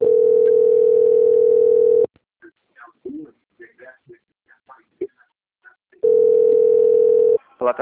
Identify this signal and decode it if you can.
Ringback tone (North America), 440 + 480 Hz, with speech.
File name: out- - - - - .37700